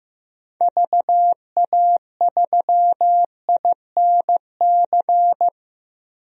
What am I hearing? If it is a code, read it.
Morse code, heard as a keyed tone, about 15 wpm, 700 Hz. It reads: VA3INC
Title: DXCC-100 Call Signs - 6 Characters - 15wpm